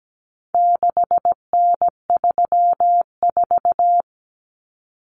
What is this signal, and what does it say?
Morse code, heard as a keyed tone, about 17 wpm, 700 Hz. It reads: 6N34
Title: Common Cut Numbers - 4-Digit 17wpm